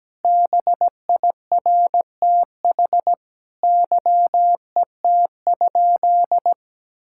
Morse code, heard as a keyed tone, about 17 wpm, 700 Hz. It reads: BIRTH YET?
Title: Sentences from Top 1000 Words - Review Part 5 - 17wpm